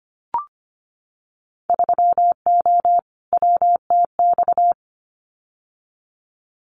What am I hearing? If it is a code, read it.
Morse code, heard as a keyed tone, about 25 wpm, 700 Hz. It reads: 3O WTX
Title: Winter Field Day Exchanges Encore 25wpm